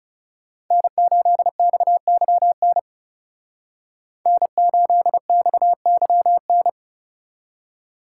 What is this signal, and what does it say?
Morse code, heard as a keyed tone, about 34 wpm, 700 Hz. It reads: N8XYN N8XYN
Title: Speed Racing - USA Call Signs 30wpm